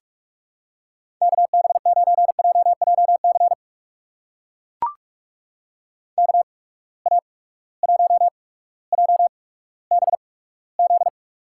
Morse code, heard as a keyed tone, about 44 wpm, spaced as for 19 wpm, 700 Hz. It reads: KB9JJC KA1JBZ
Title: USA Call Signs - 6 Characters - 8x Character Spacing - 45wpm